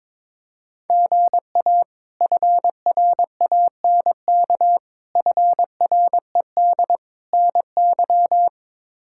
Morse code, heard as a keyed tone, about 21 wpm, 700 Hz. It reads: GA FRANK FRED NY
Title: K1USN SST QSO Practice - 22wpm